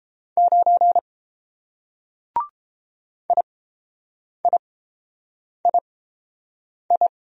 Morse code, heard as a keyed tone, about 29 wpm, spaced as for 7 wpm, 700 Hz. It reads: 9 IIII